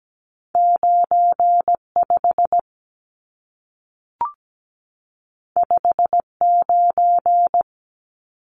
Morse code, heard as a keyed tone, about 17 wpm, 700 Hz. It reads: 95 59
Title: Numbers - 2-Digit 17wpm